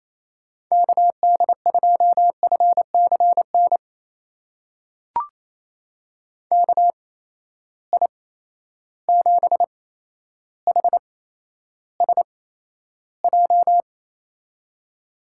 Morse code, heard as a keyed tone, about 27 wpm, spaced as for 12 wpm, 700 Hz. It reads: KD2FCN KI7HSJ
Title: USA Call Signs - 6 Characters - 8x Character Spacing - 28wpm